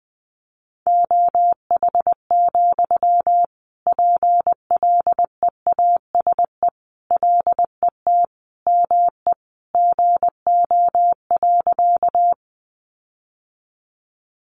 Morse code, heard as a keyed tone, about 20 wpm, 700 Hz. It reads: OH, PLEASE LET ME GO.
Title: Sentences from Top 1500 Words 20wpm